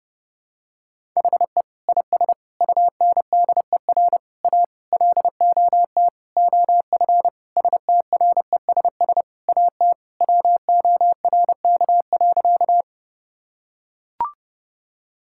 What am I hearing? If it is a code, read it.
Morse code, heard as a keyed tone, about 30 wpm, 700 Hz. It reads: HE IS UNDER A LOT OF STRESS AT WORK.
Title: Sentences from Top 1000 Words - Review Part 6 - 30wpm